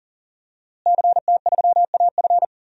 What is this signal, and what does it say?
Morse code, heard as a keyed tone, about 37 wpm, 700 Hz. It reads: CT3AF
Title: DXCC-100 Call Signs - 5 Characters - Repeated 2x - 40wpm